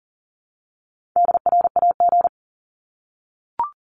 Morse code, heard as a keyed tone, about 40 wpm, 700 Hz. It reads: DRAG
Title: Top 1251-1500 Words 40wpm